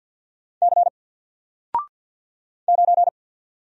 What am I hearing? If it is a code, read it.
Morse code, heard as a keyed tone, about 50 wpm, 700 Hz. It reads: C 9
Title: Single Letter-Number Alt 50wpm